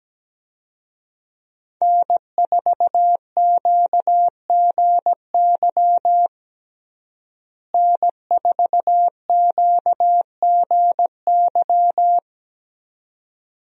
Morse code, heard as a keyed tone, about 17 wpm, 700 Hz. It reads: N4QGY N4QGY
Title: USA Call Signs - 5 Characters - Repeated 2x - 17wpm